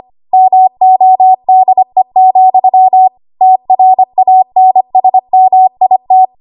Morse code, heard as a keyed tone, about 25 wpm, 750 Hz. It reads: MODE, TRANSMIT
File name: Archive / 25 WPM / 25WPM